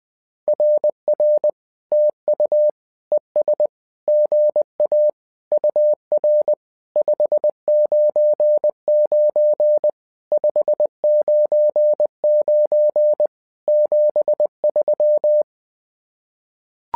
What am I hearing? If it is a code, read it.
Morse code, heard as a keyed tone, about 20 wpm, 600 Hz. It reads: RR TU ES GA UR 599 599 73
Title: SOTA QSO Practice 20wpm